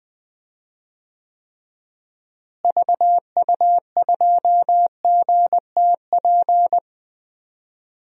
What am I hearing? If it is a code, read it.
Morse code, heard as a keyed tone, about 21 wpm, 700 Hz. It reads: VU2GTP
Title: DXCC-100 Call Signs - 6 Characters - Repeated 3x - 20wpm